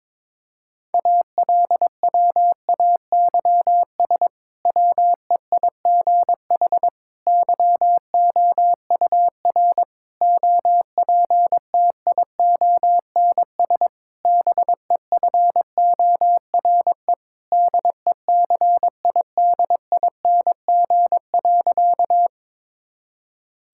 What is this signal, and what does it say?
Morse code, heard as a keyed tone, about 22 wpm, 700 Hz. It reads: ALWAYS WEIGH YOUR OPTIONS BEFORE DECIDING.
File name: Difficult Sentences with Hints - 22wpm